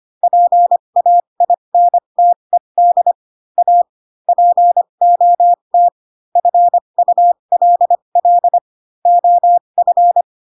Morse code, heard as a keyed tone, about 25 wpm, spaced as for 22 wpm, 700 Hz. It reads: PAINTED A POT FULL OF
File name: Pinocchio / Pinocchio ch3 18 25 ik1hge